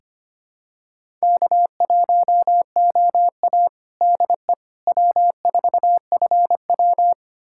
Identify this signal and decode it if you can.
Morse code, heard as a keyed tone, about 25 wpm, 700 Hz. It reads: K1OA DE W4FW
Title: USA Callsign DE Callsign - 4 Characters - 25wpm